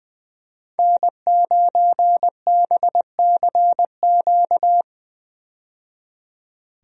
Morse code, heard as a keyed tone, about 20 wpm, 700 Hz. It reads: N9BCQ
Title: USA Call Signs - 5 Characters - 20wpm